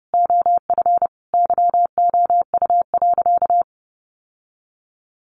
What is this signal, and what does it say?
Morse code, heard as a keyed tone, about 30 wpm, 700 Hz. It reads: OF YOU.
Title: Sentences from Top 1200 Words 30wpm